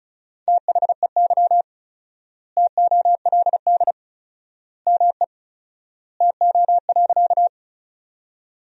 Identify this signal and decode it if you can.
Morse code, heard as a keyed tone, about 34 wpm, 700 Hz. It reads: THEY TOLD ME TO.
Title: 4x Word-Spacing - Sentences from Top 400 Words - 35wpm